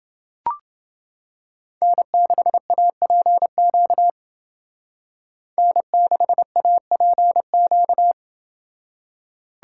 Morse code, heard as a keyed tone, about 31 wpm, 700 Hz. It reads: N6APQ N6APQ